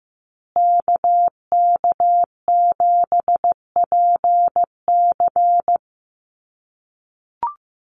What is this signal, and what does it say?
Morse code, heard as a keyed tone, about 15 wpm, 700 Hz. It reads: KK7PC
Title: USA Call Signs Repeated 3x 15wpm